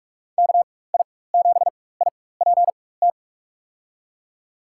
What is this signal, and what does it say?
Morse code, heard as a keyed tone, about 43 wpm, spaced as for 21 wpm, 700 Hz. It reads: KI7IPT
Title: USA Call Signs - 6 Characters - 4x Character Spacing - 45wpm